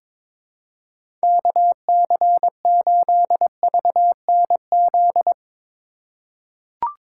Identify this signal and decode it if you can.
Morse code, heard as a keyed tone, about 23 wpm, 700 Hz. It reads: KC8VNZ
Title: USA Call Signs - 6 Characters - 2x Character Spacing - 22wpm